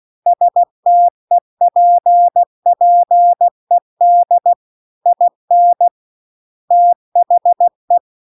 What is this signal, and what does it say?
Morse code, heard as a keyed tone, about 16 wpm, 700 Hz. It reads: STEPPED IN THE
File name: CW-The-Big-Twoheaded-River-2-16WPM